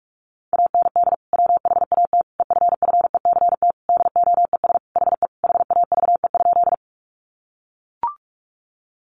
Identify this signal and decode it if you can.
Morse code, heard as a keyed tone, about 44 wpm, 700 Hz. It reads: AND WHAT EFFECT DOES HE HAVE?
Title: Sentences from Top 1000 Words 45wpm